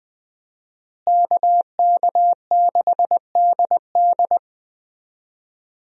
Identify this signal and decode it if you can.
Morse code, heard as a keyed tone, about 20 wpm, 700 Hz. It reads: KK6DD
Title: USA Call Signs - 5 Characters - Repeated 2x - 20wpm